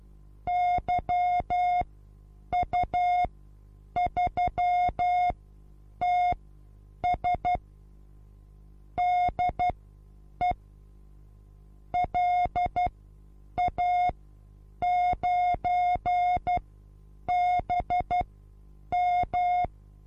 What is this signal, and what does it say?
Morse code, heard as a keyed tone, about 12 wpm, spaced as for 8 wpm, 720 Hz. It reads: YU3TS DE LA9BM
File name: repos / Morsekurs / DJ4UF Morsen23